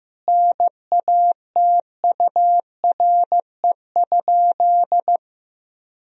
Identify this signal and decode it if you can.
Morse code, heard as a keyed tone, about 16 wpm, 700 Hz. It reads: NATURE?
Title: Sentences from Top 1000 Words - Review Part 4 - 15wpm